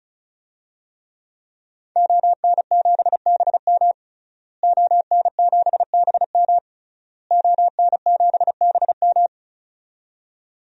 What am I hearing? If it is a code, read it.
Morse code, heard as a keyed tone, about 36 wpm, 700 Hz. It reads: ON7BM ON7BM ON7BM
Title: DXCC-100 Call Signs - 5 Characters - Repeated 3x - 35wpm